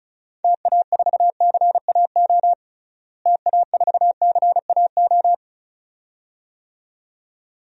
Morse code, heard as a keyed tone, about 36 wpm, 700 Hz. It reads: TA4CAO TA4CAO
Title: DXCC-100 Call Signs - 6 Characters - Repeated 2x - 35wpm